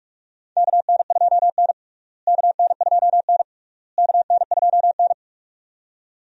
Morse code, heard as a keyed tone, about 44 wpm, 700 Hz. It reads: KN2N KN2N KN2N
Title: USA Call Signs - 4 Characters - Repeated 3x - 45wpm